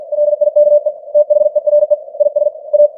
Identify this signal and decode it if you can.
Morse code, heard as a keyed tone, about 50 wpm, 590 Hz. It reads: LIKE THERE IS A
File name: archived / CWBros